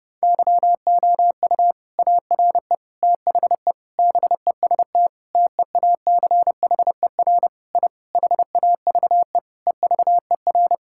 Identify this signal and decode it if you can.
Morse code, heard as a keyed tone, about 32 wpm, 700 Hz. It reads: YOU ARE THE BEST TEACHER I HAVE EVER
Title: Sentences from Top 1000 Words - Review Part 6 - 30wpm